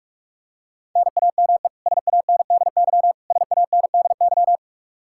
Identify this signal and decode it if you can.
Morse code, heard as a keyed tone, about 41 wpm, 700 Hz. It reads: NAME SANDY SANDY
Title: K1USN SST QSO Practice 45wpm